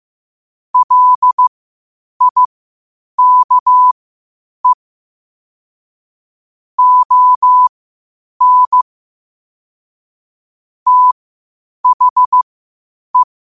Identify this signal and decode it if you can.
Morse code, heard as a keyed tone, about 15 wpm, spaced as for 8 wpm, 1000 Hz. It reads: LIKE ON THE
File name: qotdfiles / 8wpm